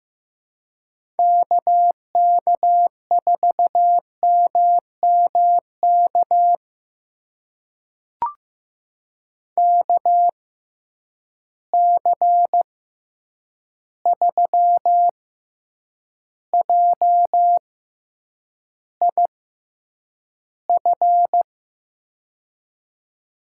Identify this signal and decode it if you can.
Morse code, heard as a keyed tone, about 15 wpm, spaced as for 8 wpm, 700 Hz. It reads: KK4MMK KC3JIF
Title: USA Call Signs - 6 Characters - 6x Character Spacing - 15wpm